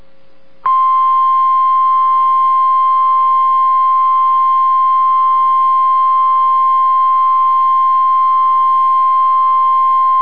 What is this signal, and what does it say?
Emergency Alert System (SAME) tones (the 1050 Hz attention tone). It is heard from 1.0 to 10.0 s.